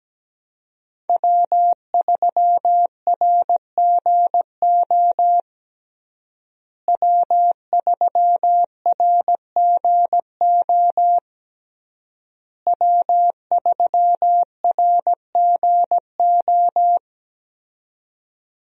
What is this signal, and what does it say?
Morse code, heard as a keyed tone, about 17 wpm, 700 Hz. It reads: W3RGO W3RGO W3RGO
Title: USA Call Signs - 5 Characters - Repeated 3x - 17wpm